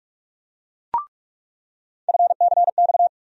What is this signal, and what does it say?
Morse code, heard as a keyed tone, about 46 wpm, 700 Hz. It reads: FCX
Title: USA Call Sign Postfixes - 3 Letters 45wpm